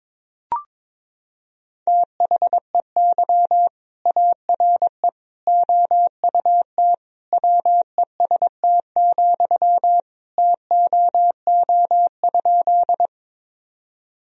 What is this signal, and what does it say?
Morse code, heard as a keyed tone, about 21 wpm, 700 Hz. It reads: THEY ARE OUT WEST, TOO?
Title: Sentences from Top 1000 Words - Review Part 2 - 22wpm